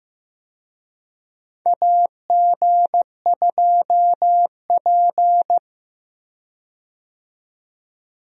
Morse code, heard as a keyed tone, about 16 wpm, 700 Hz. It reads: AG2P
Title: USA Call Signs - 4 Characters - 15wpm